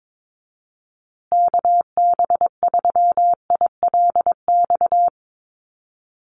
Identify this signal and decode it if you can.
Morse code, heard as a keyed tone, about 22 wpm, 700 Hz. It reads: KB3ILX
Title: USA Call Signs Repeated 3x 22wpm